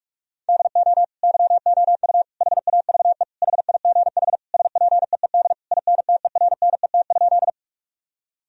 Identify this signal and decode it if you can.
Morse code, heard as a keyed tone, about 41 wpm, 700 Hz. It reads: DO YOU HAVE HIGH SPEED INTERNET?
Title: Sentences from Top 1000 Words - Review Part 4 - 45wpm